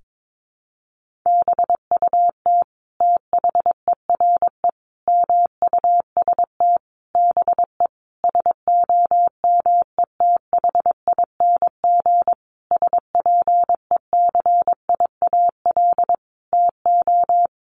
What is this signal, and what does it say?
Morse code, heard as a keyed tone, about 22 wpm, 700 Hz. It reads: BUT THERE MUST BE SOMETHING SPECIAL TO
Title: Sentences from Top 1500 Words 22wpm